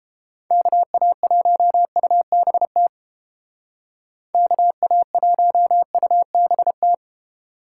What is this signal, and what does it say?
Morse code, heard as a keyed tone, about 31 wpm, 700 Hz. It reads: KA1UBT KA1UBT